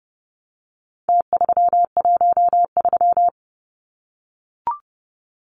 Morse code, heard as a keyed tone, about 30 wpm, 700 Hz. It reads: T313
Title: Numbers - Contest Style - 4-Digit 30wpm